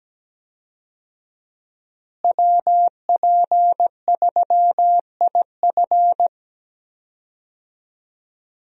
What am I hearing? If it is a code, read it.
Morse code, heard as a keyed tone, about 17 wpm, 700 Hz. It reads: WP3IF
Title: USA Call Signs - 5 Characters - Repeated 2x - 17wpm